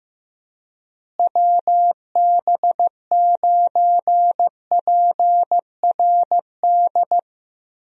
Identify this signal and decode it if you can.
Morse code, heard as a keyed tone, about 15 wpm, 700 Hz. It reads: WB9PRD
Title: USA Call Signs - 6 Characters - Repeated 2x - 15wpm